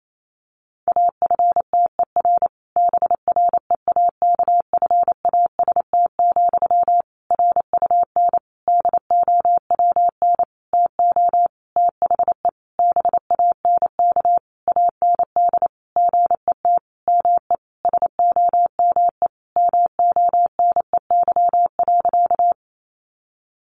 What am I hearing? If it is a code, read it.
Morse code, heard as a keyed tone, about 28 wpm, 700 Hz. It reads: AFTER BREAKFAST, RUN DOWN TO THE BANK AND GET ME SOME MONEY.
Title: Sentences from Top 1500 Words 28wpm